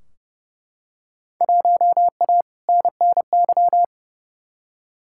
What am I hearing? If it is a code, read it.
Morse code, heard as a keyed tone, about 30 wpm, 700 Hz. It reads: 1A NNY
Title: Field Day Exchanges - 30wpm